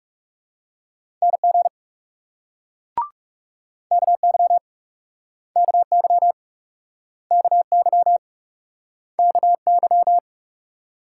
Morse code, heard as a keyed tone, about 39 wpm, 700 Hz. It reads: NG KY KY KY KY